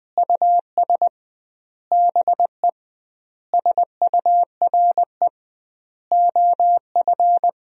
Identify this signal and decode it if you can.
Morse code, heard as a keyed tone, about 20 wpm, 700 Hz. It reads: US BE SURE OF
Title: 2x Word-Spacing - Sentences from Top 300 Words - 20wpm